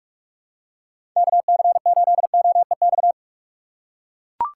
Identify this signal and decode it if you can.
Morse code, heard as a keyed tone, about 44 wpm, 700 Hz. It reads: KC8OEX
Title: USA Call Signs - 6 Characters - 6x Character Spacing - 45wpm